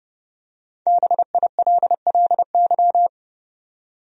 Morse code, heard as a keyed tone, about 30 wpm, 700 Hz. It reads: BILLY
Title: USA Common Names - 30wpm